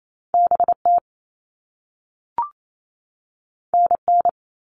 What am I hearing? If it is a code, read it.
Morse code, heard as a keyed tone, about 28 wpm, 700 Hz. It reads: BT NN
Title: All Cut Numbers - 2-Digit 28wpm